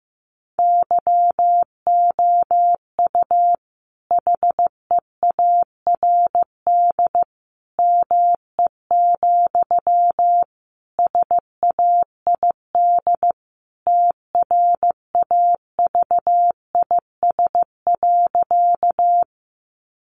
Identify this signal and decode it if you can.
Morse code, heard as a keyed tone, about 15 wpm, 700 Hz. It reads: YOU HEARD ME, SAID TRAVIS.
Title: A Sound Of Thunder 15wpm